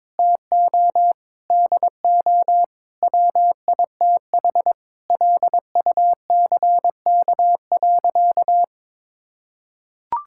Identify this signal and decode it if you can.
Morse code, heard as a keyed tone, about 22 wpm, 700 Hz. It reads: TO DO WITH LUCK.
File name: practice / 22wpm / Sentences with 8-Word Phrases - 22wpm